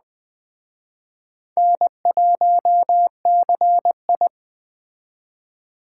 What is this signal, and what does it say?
Morse code, heard as a keyed tone, about 20 wpm, 700 Hz. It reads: N1CI
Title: Speed Racing - USA 1x2 Call Signs 17wpm